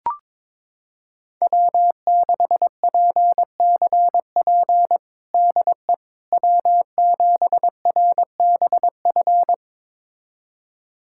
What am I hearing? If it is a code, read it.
Morse code, heard as a keyed tone, about 22 wpm, 700 Hz. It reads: W6PCP DE W7RBF
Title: USA Callsign DE Callsign - 5 Characters - 22wpm